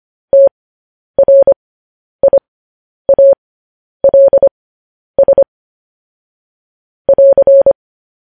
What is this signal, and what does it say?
Morse code, heard as a keyed tone, about 25 wpm, spaced as for 10 wpm, 550 Hz. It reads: TRIALS <AR>